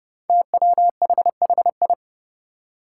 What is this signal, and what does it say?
Morse code, heard as a keyed tone, about 30 wpm, 700 Hz. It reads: TWHHI
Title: Five Challenging Letter-Number Combinations 30wpm